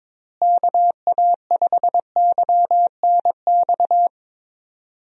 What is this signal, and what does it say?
Morse code, heard as a keyed tone, about 22 wpm, 700 Hz. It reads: KA5YNX
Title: USA Call Signs - 6 Characters - Repeated 2x - 22wpm